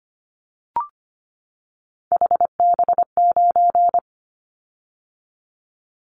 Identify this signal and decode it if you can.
Morse code, heard as a keyed tone, about 25 wpm, 700 Hz. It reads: HB9
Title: Callsign Prefixes for Top 25 DXCC - 25wpm